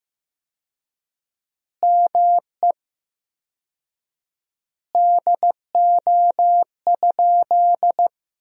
Morse code, heard as a keyed tone, about 15 wpm, 700 Hz. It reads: ME DO?
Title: 4x Word-Spacing - Sentences from Top 200 Words - 15wpm